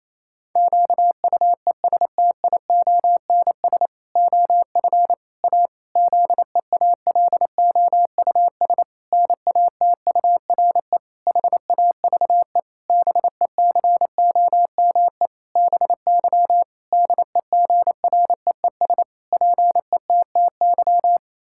Morse code, heard as a keyed tone, about 28 wpm, 700 Hz. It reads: QUESTIONS OF A ZEALOUS NATURE HAVE BECOME BY DEGREES PETTY
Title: Pangram Challenge v2 - 28wpm